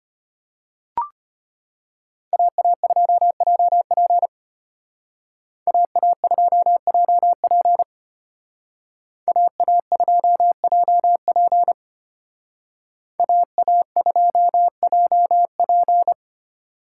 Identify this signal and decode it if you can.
Morse code, heard as a keyed tone, about 32 wpm, 700 Hz. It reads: AA2JP AA2JP AA2JP AA2JP